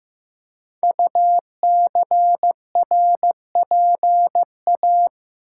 Morse code, heard as a keyed tone, about 15 wpm, 700 Hz. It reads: UCRPA
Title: Code Groups - 5 Letter:Numbers - 15wpm